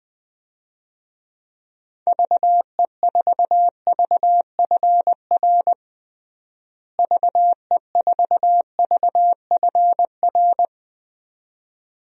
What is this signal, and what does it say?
Morse code, heard as a keyed tone, about 20 wpm, 700 Hz. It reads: VE4VFR VE4VFR
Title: DXCC-100 Call Signs - 6 Characters - Repeated 2x - 20wpm